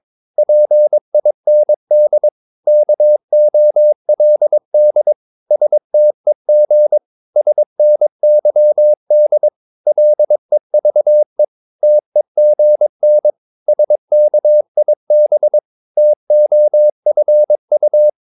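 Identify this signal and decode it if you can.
Morse code, heard as a keyed tone, about 22 wpm, 600 Hz. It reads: PIND KOLD STEG SNYD LEVE TEGN SKIB TOFU
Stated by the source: Morse practice of short Danish words at 22 wpm, from CW med Gnister.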